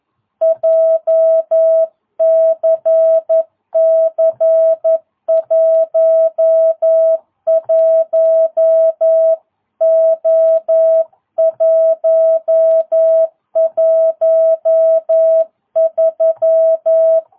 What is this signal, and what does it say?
Morse code, heard as a keyed tone, about 11 wpm, 660 Hz. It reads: JCC 11O113
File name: jh8cbh / 2022cw